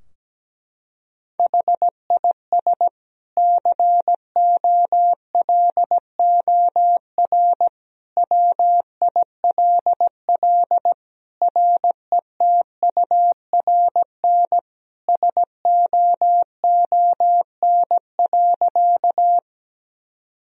Morse code, heard as a keyed tone, about 17 wpm, 700 Hz. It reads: HIS COLOR WILL RETURN SOON.
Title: Sentences from Top 1000 Words - Review Part 2 - 17wpm